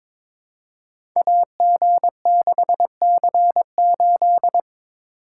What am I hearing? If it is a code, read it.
Morse code, heard as a keyed tone, about 22 wpm, 700 Hz. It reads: AG6C8
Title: Code Groups - 5 Letter:Numbers - 22wpm